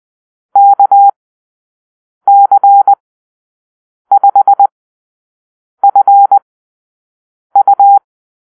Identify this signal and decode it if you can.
Morse code, heard as a keyed tone, about 20 wpm, spaced as for 6 wpm, 800 Hz. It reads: KC5FU
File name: arrl / Call-Sign Drill-6